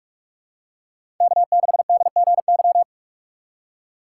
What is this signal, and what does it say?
Morse code, heard as a keyed tone, about 42 wpm, 700 Hz. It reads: K6DGY
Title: USA Call Signs - 5 Characters - 3x Character Spacing - 45wpm